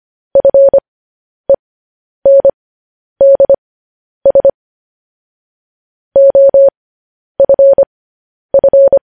Morse code, heard as a keyed tone, about 25 wpm, spaced as for 10 wpm, 550 Hz. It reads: FENDS OFF
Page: News Headlines in Morse Code at 10 WPM